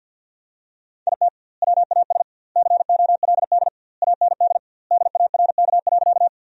Morse code, heard as a keyed tone, about 50 wpm, 700 Hz. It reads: IT WAS COLD AND DARK.
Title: 2x Word-Spacing - Sentences from Top 500 Words - 50wpm